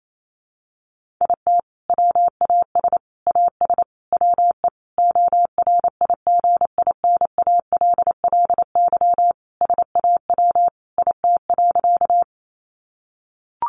Morse code, heard as a keyed tone, about 28 wpm, 700 Hz. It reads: IT WAS AS WE ORIGINALLY SAW IT.
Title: Sentences from Top 1500 Words 28wpm